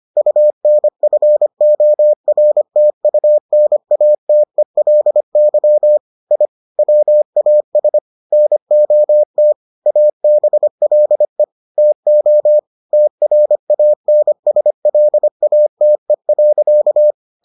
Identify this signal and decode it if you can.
Morse code, heard as a keyed tone, about 25 wpm, 600 Hz. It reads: UNFORTUNATELY I WAS NOT ABLE TO TRANSLATE.